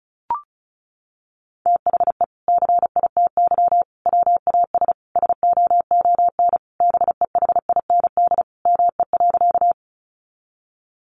Morse code, heard as a keyed tone, about 34 wpm, 700 Hz. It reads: THE CITY WAS SOON BEHIND ME.
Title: Sentences from Top 500 Words 35wpm